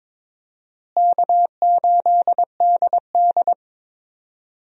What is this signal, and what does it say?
Morse code, heard as a keyed tone, about 21 wpm, 700 Hz. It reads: K8DD
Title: USA Call Signs - 4 Characters - Repeated 3x - 22wpm